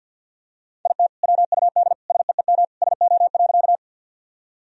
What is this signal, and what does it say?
Morse code, heard as a keyed tone, about 46 wpm, 700 Hz. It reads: IT WUD SEEM SO.
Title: Sentences from Top 300 Words and Abbreviations 50wpm